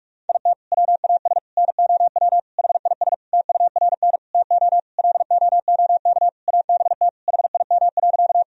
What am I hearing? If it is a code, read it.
Morse code, heard as a keyed tone, about 46 wpm, 700 Hz. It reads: IT WAS NOW HIS TURN TO LOOK ABT HIM.
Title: Sentences from Top 200 Words and Abbreviations 45wpm